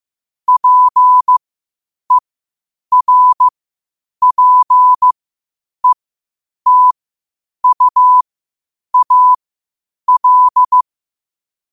Morse code, heard as a keyed tone, about 15 wpm, spaced as for 8 wpm, 1000 Hz. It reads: PERPETUAL
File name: qotdfiles / 8wpm